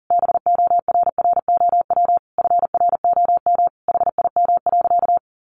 Morse code, heard as a keyed tone, about 40 wpm, 700 Hz. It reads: BORROW FROM HIM.
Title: Sentences from Top 1500 Words 40wpm